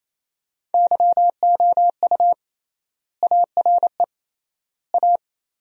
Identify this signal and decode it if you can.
Morse code, heard as a keyed tone, about 28 wpm, 700 Hz. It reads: YOU ARE A
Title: 3x Word-Spacing - Sentences from Top 400 Words - 28wpm